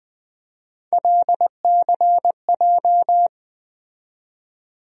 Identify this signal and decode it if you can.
Morse code, heard as a keyed tone, about 20 wpm, 700 Hz. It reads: LCJ
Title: USA Call Sign Postfixes - 3 Letters 20wpm